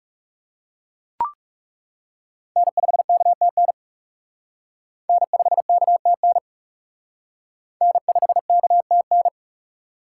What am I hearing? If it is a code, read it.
Morse code, heard as a keyed tone, about 36 wpm, 700 Hz. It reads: N5KTN N5KTN N5KTN
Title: Speed Racing - USA 1x3 Call Signs 30wpm